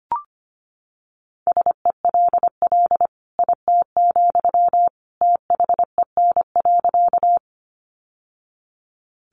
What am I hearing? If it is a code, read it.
Morse code, heard as a keyed tone, about 25 wpm, 700 Hz. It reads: SELL IT, THEN.
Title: Sentences from Top 1100 Words 25wpm